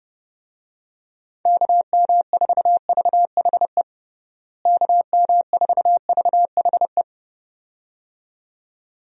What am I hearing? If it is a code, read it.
Morse code, heard as a keyed tone, about 30 wpm, 700 Hz. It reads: KM4VHE KM4VHE
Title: USA Call Signs - 6 Characters - Repeated 2x - 30wpm